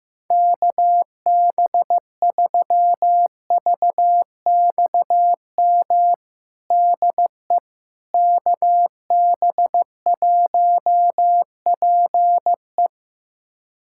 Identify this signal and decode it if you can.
Morse code, heard as a keyed tone, about 15 wpm, 700 Hz. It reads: KB3VXM DE KB1PE
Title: USA Callsign DE Callsign 15wpm